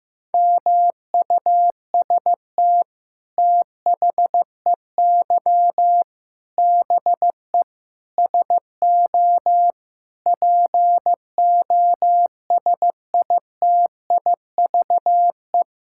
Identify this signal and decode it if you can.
Morse code, heard as a keyed tone, about 15 wpm, 700 Hz. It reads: MUST THEY BE SO POSITIVE
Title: Sentences from Top 1000 Words - Review Part 5 - 15wpm